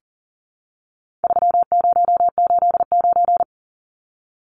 Morse code, heard as a keyed tone, about 40 wpm, 700 Hz. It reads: 3089
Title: Numbers - 4-Digit 40wpm